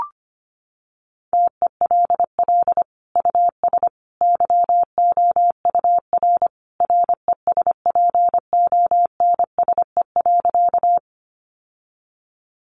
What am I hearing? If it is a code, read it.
Morse code, heard as a keyed tone, about 25 wpm, 700 Hz. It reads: TELL US YOUR RESPONSE.
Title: Sentences from Top 1400 Words 25wpm